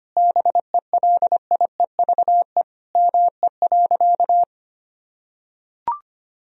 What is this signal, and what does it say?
Morse code, heard as a keyed tone, about 25 wpm, 700 Hz. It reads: BELIEVE ME.
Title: Sentences from Top 1000 Words - Review Part 3 - 25wpm